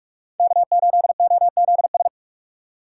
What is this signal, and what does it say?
Morse code, heard as a keyed tone, about 40 wpm, 700 Hz. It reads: K8OZS